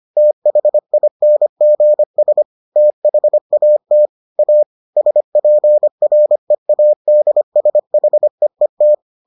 Morse code, heard as a keyed tone, about 25 wpm, 600 Hz. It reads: THINGS THAT A SPREADSHEET